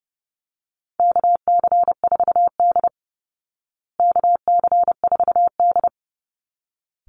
Morse code, heard as a keyed tone, about 28 wpm, 700 Hz. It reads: KC4D KC4D
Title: Call Signs Repeated 3x - 30wpm